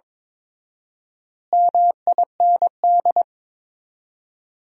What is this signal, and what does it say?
Morse code, heard as a keyed tone, about 23 wpm, 700 Hz. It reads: MIND
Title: 4x Word-Spacing - Sentences from Top 500 Words - 22wpm